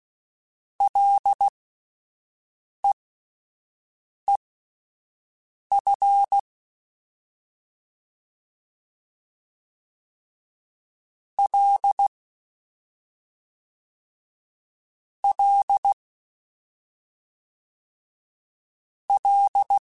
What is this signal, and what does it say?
Morse code, heard as a keyed tone, about 16 wpm, spaced as for 5 wpm, 800 Hz. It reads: LEEF L L L
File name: Ls / Single L m3s / Single L Prefixe 3x2 Twice @5